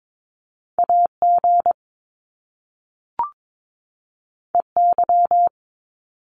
Morse code, heard as a keyed tone, about 22 wpm, 700 Hz. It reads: AG EY
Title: Top Two Letter Combinations in Words 22wpm